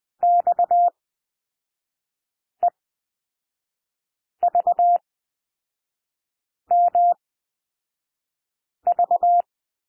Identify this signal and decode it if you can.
Morse code, heard as a keyed tone, about 20 wpm, spaced as for 5 wpm, 700 Hz. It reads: XEVMV
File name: SOUND LETTERS 20 wpm